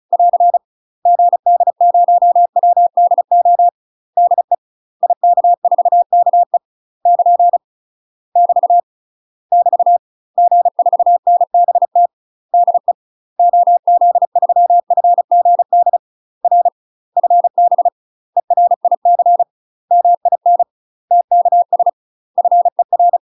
Morse code, heard as a keyed tone, about 36 wpm, 700 Hz. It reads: <AR> GD0WDO DE IK4KE <KN> <BT> <BT> G4NBT DE OZ3FGD R FB ERIC MIN TKS FER
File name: UK MASTER QSOs TRAINING / 35WPM / UK MASTER QSOs 0012